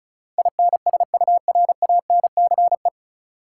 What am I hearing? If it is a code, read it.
Morse code, heard as a keyed tone, about 36 wpm, 700 Hz. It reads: INSURANCE
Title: Sentences from Top 1100 Words - New Word Drill - 35wpm